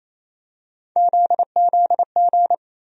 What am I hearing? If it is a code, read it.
Morse code, heard as a keyed tone, about 28 wpm, 700 Hz. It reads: ZZG
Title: USA Call Sign Postfixes - 3 Letters 28wpm